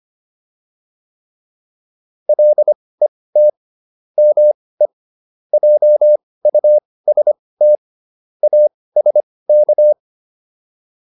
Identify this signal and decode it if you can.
Morse code, heard as a keyed tone, about 25 wpm, spaced as for 18 wpm, 600 Hz. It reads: LET ME JUST ASK